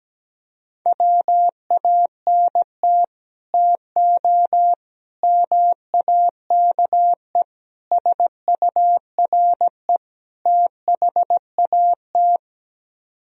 Morse code, heard as a keyed tone, about 17 wpm, 700 Hz. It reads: WANT TO MAKE SURE THAT
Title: Speed-Racing - Sets of 5 Words 17wpm